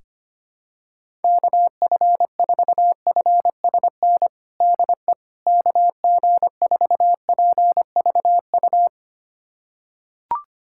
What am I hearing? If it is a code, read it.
Morse code, heard as a keyed tone, about 25 wpm, 700 Hz. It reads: KF4FSN DE KG4PVU
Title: USA Callsign DE Callsign - 6 Characters - 25wpm